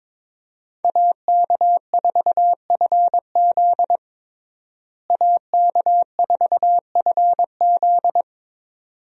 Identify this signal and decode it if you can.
Morse code, heard as a keyed tone, about 21 wpm, 700 Hz. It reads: AK4FZ AK4FZ
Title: USA Call Signs - 5 Characters - Repeated 2x - 22wpm